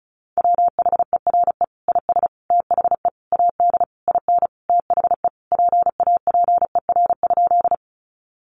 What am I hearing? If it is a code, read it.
Morse code, heard as a keyed tone, about 34 wpm, 700 Hz. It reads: WHERE IS THE AD IN THE PAPER?
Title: Sentences from Top 1300 Words 35wpm